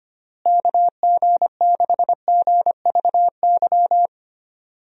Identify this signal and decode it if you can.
Morse code, heard as a keyed tone, about 25 wpm, 700 Hz. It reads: KG6GVY
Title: USA Call Signs - 6 Characters - Repeated 3x - 25wpm